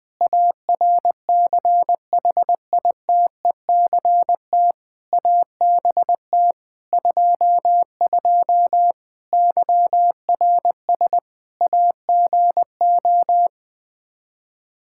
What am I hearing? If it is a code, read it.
Morse code, heard as a keyed tone, about 20 wpm, 700 Hz. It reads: ARCHITECT ABT 22 YRS AGO
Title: Practice QSO - Exchange 3 - More Info - 20wpm